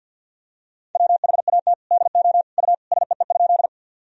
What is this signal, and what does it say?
Morse code, heard as a keyed tone, about 51 wpm, 700 Hz. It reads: WHAT DO U SEE?
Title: Sentences from Top 100 Words and Abbreviations 50wpm